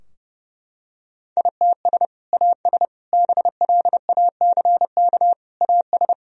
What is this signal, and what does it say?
Morse code, heard as a keyed tone, about 32 wpm, 700 Hz. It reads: ITS AS BLACK AS
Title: Sentences from Top 1000 Words - Review Part 2 - 30wpm